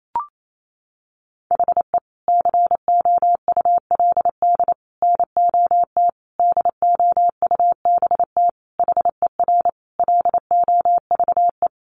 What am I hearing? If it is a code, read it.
Morse code, heard as a keyed tone, about 28 wpm, 700 Hz. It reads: HE COULD NOT DOUBT HER LOVE
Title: Sentences from Top 1500 Words 28wpm